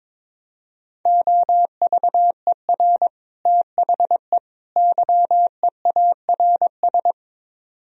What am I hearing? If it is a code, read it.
Morse code, heard as a keyed tone, about 22 wpm, 700 Hz. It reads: OVER THE YEARS
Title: Speed-Racing - Sets of 3 Words 22wpm